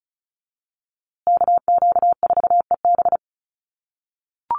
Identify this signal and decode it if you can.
Morse code, heard as a keyed tone, about 34 wpm, 700 Hz. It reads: KQ4EB
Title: USA Call Signs Repeated 3x 35wpm